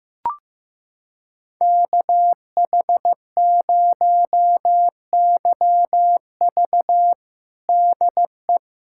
Morse code, heard as a keyed tone, about 15 wpm, 700 Hz. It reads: KH0YV DE
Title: USA Callsign DE Callsign - 5 Characters - 15wpm